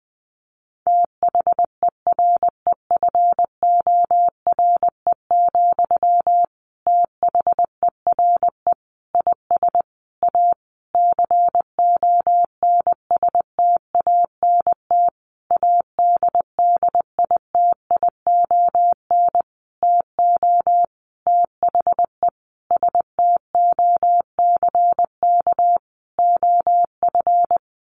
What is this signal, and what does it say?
Morse code, heard as a keyed tone, about 20 wpm, 700 Hz. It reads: THEREFORE, THERE IS A CONSTANT ADDITION TO THE STOCK OF
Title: Sentences from Top 1500 Words 20wpm